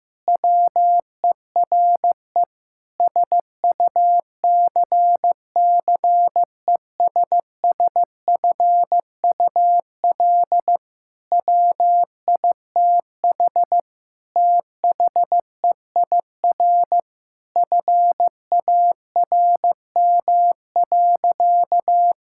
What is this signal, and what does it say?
Morse code, heard as a keyed tone, about 15 wpm, 700 Hz. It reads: WERE SUCCESSFUL WITH THEIR FARM.
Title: Sentences from Top 1000 Words - Review Part 6 - 15wpm